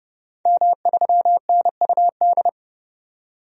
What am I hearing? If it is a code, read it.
Morse code, heard as a keyed tone, about 31 wpm, 690 Hz. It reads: M3NUD
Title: DXCC-100 Call Signs - 5 Characters - 30wpm